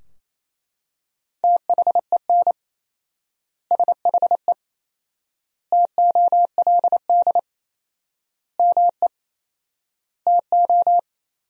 Morse code, heard as a keyed tone, about 28 wpm, 700 Hz. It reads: THEN SHE TOLD ME TO
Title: 4x Word-Spacing - Sentences from Top 400 Words - 28wpm